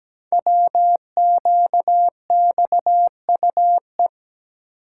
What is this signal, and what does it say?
Morse code, heard as a keyed tone, about 17 wpm, 700 Hz. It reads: WQXUE
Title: Code Groups - 5 Letters - 17wpm